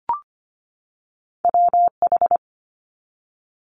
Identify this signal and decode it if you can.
Morse code, heard as a keyed tone, about 25 wpm, 700 Hz. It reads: WH
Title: Multi-letter Phonemes 25wpm